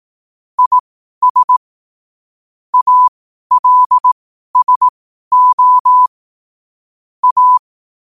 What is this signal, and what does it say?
Morse code, heard as a keyed tone, about 18 wpm, spaced as for 12 wpm, 1000 Hz. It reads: IS ALSO A
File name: qotdfiles / 12wpm